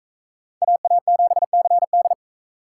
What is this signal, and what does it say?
Morse code, heard as a keyed tone, about 44 wpm, 700 Hz. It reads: AA7CD